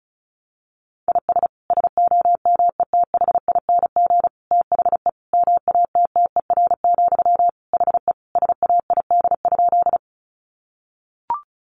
Morse code, heard as a keyed tone, about 34 wpm, 700 Hz. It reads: IS SOMETHING THE MATTER, HE SAID?
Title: Sentences from Top 700 Words 35wpm